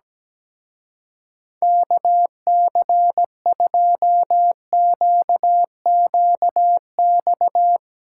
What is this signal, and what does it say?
Morse code, heard as a keyed tone, about 19 wpm, 700 Hz. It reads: KC2QQX
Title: Speed Racing - USA 2x3 Call Signs 17wpm